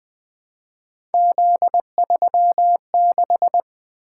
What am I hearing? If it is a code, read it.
Morse code, heard as a keyed tone, about 20 wpm, 700 Hz. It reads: Z36
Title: Three Letter-Number Combinations - 20wpm